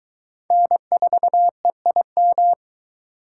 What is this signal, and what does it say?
Morse code, heard as a keyed tone, about 20 wpm, 700 Hz. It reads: N4EIM